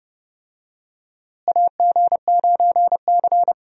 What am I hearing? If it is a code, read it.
Morse code, heard as a keyed tone, about 29 wpm, 700 Hz. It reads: AG9C
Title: USA Call Signs - 4 Characters - Repeated 2x - 30wpm